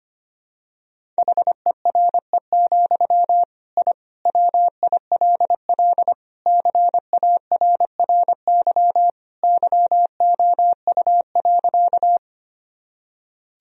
Morse code, heard as a keyed tone, about 25 wpm, 700 Hz. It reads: HERE, I WILL CARRY YOU.
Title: Sentences from Top 1000 Words - Review Part 1 - 25wpm